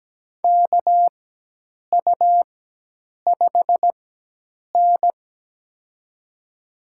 Morse code, heard as a keyed tone, about 17 wpm, spaced as for 8 wpm, 700 Hz. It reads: KU5N
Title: USA Call Signs - 4x Character Spacing - 17wpm